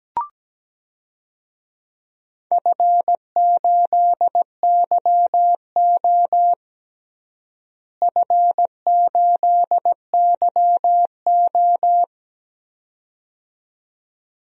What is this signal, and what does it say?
Morse code, heard as a keyed tone, about 17 wpm, 700 Hz. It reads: F8YO F8YO
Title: DXCC-100 Call Signs - 4 Characters - Repeated 2x - 17wpm